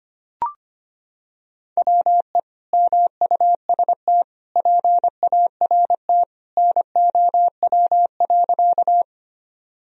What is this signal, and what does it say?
Morse code, heard as a keyed tone, about 25 wpm, 700 Hz. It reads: WE MUST PART NOW.
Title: Sentences from Top 1000 Words - Review Part 1 - 25wpm